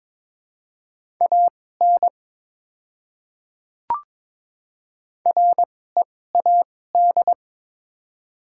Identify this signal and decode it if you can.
Morse code, heard as a keyed tone, about 23 wpm, spaced as for 16 wpm, 700 Hz. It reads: AN READ
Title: 2x Character Spacing - Top 500 Words - 22wpm